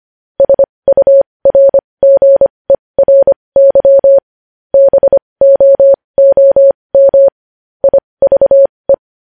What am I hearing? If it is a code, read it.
Morse code, heard as a keyed tone, about 25 wpm, spaced as for 20 wpm, 550 Hz. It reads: SURGERY BOOM IVE
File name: podcast / News headlines-1x-25x20